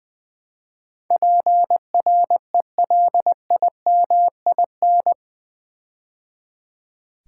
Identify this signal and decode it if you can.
Morse code, heard as a keyed tone, about 20 wpm, 700 Hz. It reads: PRELIMIN
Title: Build A Word - 4 Letters At A Time - 11-Letter Words - 20wpm